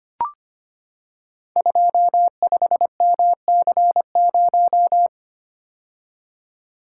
Morse code, heard as a keyed tone, about 25 wpm, 700 Hz. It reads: 25MC0
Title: Code Groups - 5 Letter:Numbers - 25wpm